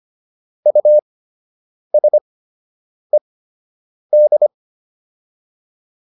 Morse code, heard as a keyed tone, about 25 wpm, spaced as for 8 wpm, 600 Hz. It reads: USED
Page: fundamental-practice-files